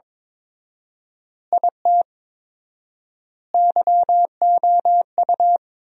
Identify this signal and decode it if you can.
Morse code, heard as a keyed tone, about 22 wpm, 700 Hz. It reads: IT YOU
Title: 4x Word-Spacing - Sentences from Top 200 Words - 22wpm